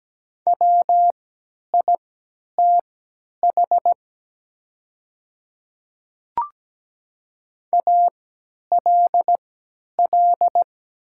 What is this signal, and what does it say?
Morse code, heard as a keyed tone, about 17 wpm, spaced as for 10 wpm, 700 Hz. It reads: WITH ALL
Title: 3x Character Spacing - Top 100 Words - 17wpm